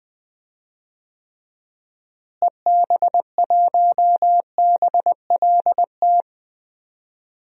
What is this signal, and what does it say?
Morse code, heard as a keyed tone, about 20 wpm, 700 Hz. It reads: EB1BLT
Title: DXCC-100 Call Signs - 6 Characters - Repeated 3x - 20wpm